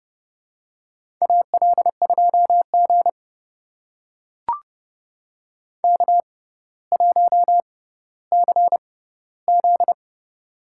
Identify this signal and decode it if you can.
Morse code, heard as a keyed tone, about 32 wpm, spaced as for 16 wpm, 700 Hz. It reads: AL2G K1CZ
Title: USA Call Signs - 4 Characters - 6x Character Spacing - 30wpm